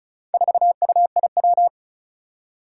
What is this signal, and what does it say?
Morse code, heard as a keyed tone, about 36 wpm, 700 Hz. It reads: 4UIW
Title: Four Challenging Letter-Number Combinations 35wpm